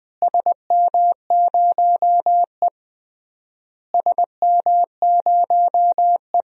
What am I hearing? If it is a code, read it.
Morse code, heard as a keyed tone, about 19 wpm, 700 Hz. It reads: SM0E SM0E
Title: DXCC-100 Call Signs - 4 Characters - Repeated 3x - 20wpm